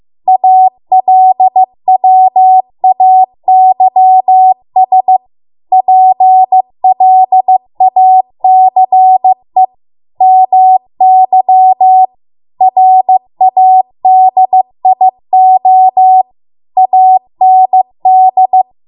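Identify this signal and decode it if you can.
Morse code, heard as a keyed tone, about 15 wpm, 750 Hz. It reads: ALWAYS PLACE MY RADIO AND
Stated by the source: Morse code practice at 15 wpm from the national amateur radio association's code archive.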